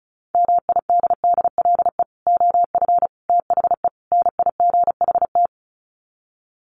Morse code, heard as a keyed tone, about 34 wpm, 700 Hz. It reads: MIDDLE OF THE NIGHT
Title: Sets of 4 Words 35wpm